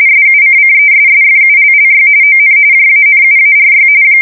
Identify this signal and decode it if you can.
RTTY, about 45.45 baud. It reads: SUCCESSFULLY ON JANUARY
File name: fsk170 / teletype-RTTY